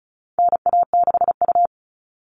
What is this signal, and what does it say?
Morse code, heard as a keyed tone, about 34 wpm, 700 Hz. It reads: NA6U
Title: All Cut Numbers - 4-Digit 35wpm